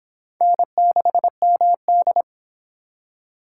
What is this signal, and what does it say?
Morse code, heard as a keyed tone, about 30 wpm, 700 Hz. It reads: N6MD